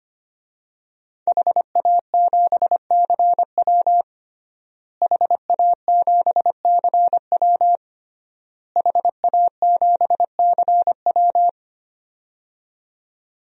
Morse code, heard as a keyed tone, about 25 wpm, 700 Hz. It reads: HA7CW HA7CW HA7CW
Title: DXCC-100 Call Signs - 5 Characters - Repeated 3x - 25wpm